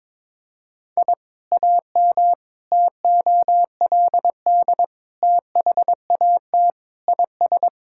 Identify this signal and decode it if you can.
Morse code, heard as a keyed tone, about 22 wpm, 700 Hz. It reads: I AM TOLD THAT IS
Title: Sentences from Top 1000 Words - Review Part 5 - 22wpm